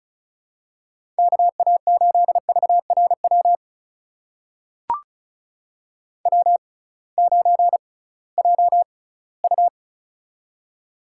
Morse code, heard as a keyed tone, about 34 wpm, spaced as for 20 wpm, 700 Hz. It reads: KA8VRW W9JU
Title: USA Call Signs - 6x Character Spacing - 35wpm